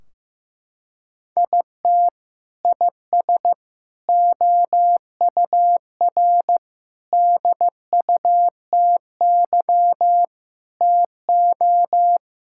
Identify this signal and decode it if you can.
Morse code, heard as a keyed tone, about 15 wpm, 700 Hz. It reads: IT IS OUR DUTY TO
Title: Sentences from Top 1600 Words 15wpm